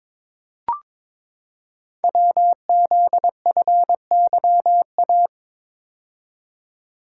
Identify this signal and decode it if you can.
Morse code, heard as a keyed tone, about 22 wpm, 700 Hz. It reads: WZFYA
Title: Code Groups - 5 Letters - 22wpm